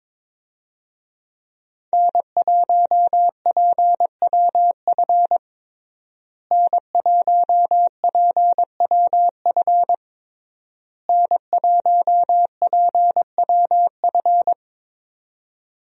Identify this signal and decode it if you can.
Morse code, heard as a keyed tone, about 21 wpm, 700 Hz. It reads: N1PWF N1PWF N1PWF
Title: USA Call Signs - 5 Characters - Repeated 3x - 22wpm